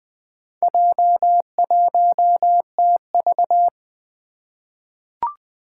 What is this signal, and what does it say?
Morse code, heard as a keyed tone, about 20 wpm, 700 Hz. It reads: J1TV
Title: Four Challenging Letter-Number Combinations 20wpm